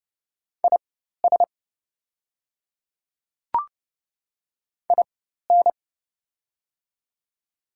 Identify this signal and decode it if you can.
Morse code, heard as a keyed tone, about 30 wpm, spaced as for 14 wpm, 700 Hz. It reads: IS IN
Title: 4x Character Spacing - Top 100 Words - 30wpm